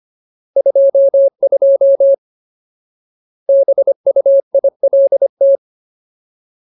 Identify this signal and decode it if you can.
Morse code, heard as a keyed tone, about 25 wpm, 550 Hz. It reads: 22 BUILT
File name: Ultimate Word List / 25WPM Ultimate